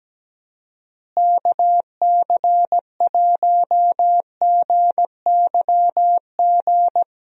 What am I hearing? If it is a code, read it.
Morse code, heard as a keyed tone, about 17 wpm, 700 Hz. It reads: KC1GYG
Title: USA Call Signs - 6 Characters - Repeated 2x - 17wpm